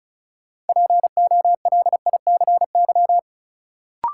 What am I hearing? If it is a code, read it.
Morse code, heard as a keyed tone, about 36 wpm, 700 Hz. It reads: POLICY
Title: Sentences from Top 1200 Words - New Word Drill Alt - 35wpm